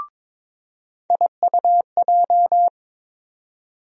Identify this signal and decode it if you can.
Morse code, heard as a keyed tone, about 22 wpm, 700 Hz. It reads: IUJ
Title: USA Call Sign Postfixes - 3 Letters 22wpm